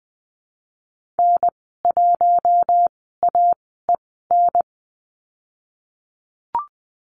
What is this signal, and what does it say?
Morse code, heard as a keyed tone, about 20 wpm, spaced as for 15 wpm, 700 Hz. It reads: N1AEN
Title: 048 Lesson 9g - Number 4 - Call Signs 20wpm